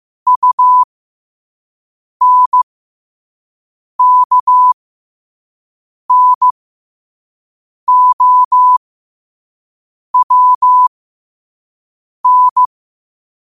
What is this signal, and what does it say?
Morse code, heard as a keyed tone, about 15 wpm, spaced as for 5 wpm, 1000 Hz. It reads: UNKNOWN
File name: qotdfiles / 5wpm